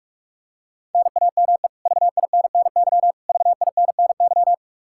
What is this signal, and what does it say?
Morse code, heard as a keyed tone, about 45 wpm, 700 Hz. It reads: NAME VINNY VINNY
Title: Practice QSO - Exchange 1 - Beginning - 45wpm